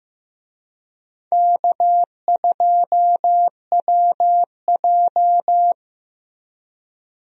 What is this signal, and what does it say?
Morse code, heard as a keyed tone, about 15 wpm, 700 Hz. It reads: K2WJ
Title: USA Call Signs - 4 Characters - 15wpm